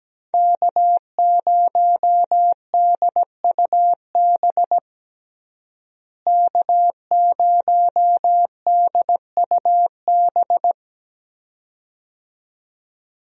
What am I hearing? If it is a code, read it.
Morse code, heard as a keyed tone, about 17 wpm, 700 Hz. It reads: K0DUB K0DUB
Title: USA Call Signs - 5 Characters - Repeated 2x - 17wpm